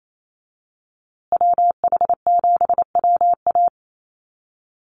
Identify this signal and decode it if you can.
Morse code, heard as a keyed tone, about 28 wpm, 700 Hz. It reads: WH7WA
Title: USA Call Signs Repeated 2x 28wpm